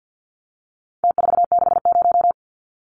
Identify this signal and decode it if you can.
Morse code, heard as a keyed tone, about 49 wpm, 700 Hz. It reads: T460
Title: Numbers - Contest Style - 4-Digit 50wpm